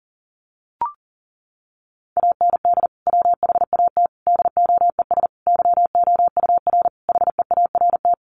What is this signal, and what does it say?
Morse code, heard as a keyed tone, about 40 wpm, 700 Hz. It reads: AND WHAT DOES YOUR HEART
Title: Sentences from Top 1200 Words 40wpm